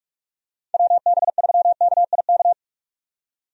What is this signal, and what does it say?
Morse code, heard as a keyed tone, about 46 wpm, 700 Hz. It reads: WB3KIK
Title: USA Call Signs - 6 Characters - 45wpm